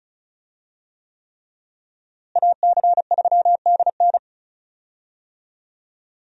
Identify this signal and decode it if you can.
Morse code, heard as a keyed tone, about 36 wpm, 700 Hz. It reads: AC3DN
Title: USA Call Signs - 5 Characters - Repeated 3x - 35wpm